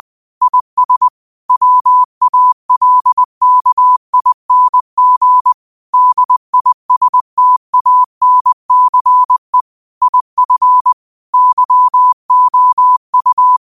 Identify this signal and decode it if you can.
Morse code, heard as a keyed tone, about 20 wpm, 1000 Hz. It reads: IS WALKING DISTANCE IF YOU